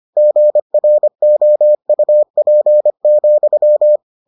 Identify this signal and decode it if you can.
Morse code, heard as a keyed tone, about 25 wpm, 600 Hz. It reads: GROUP,